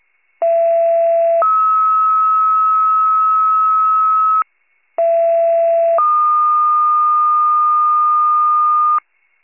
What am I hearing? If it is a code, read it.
Two-tone page, heard as a one-second tone, then a three-second tone, about 670 then 1290 Hz, about 670 then 1220 Hz. It is heard from 0.4 to 9.0 s.
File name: pager / 39 RESCUE / wcalert